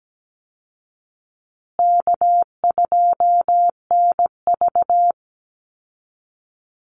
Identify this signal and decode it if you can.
Morse code, heard as a keyed tone, about 17 wpm, 700 Hz. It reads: K2NV
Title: Call Signs Repeated 2x - 17wpm